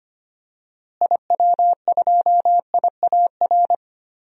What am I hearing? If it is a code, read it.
Morse code, heard as a keyed tone, about 25 wpm, 700 Hz. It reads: IW2IAR
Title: DXCC-100 Call Signs - 6 Characters - Repeated 2x - 25wpm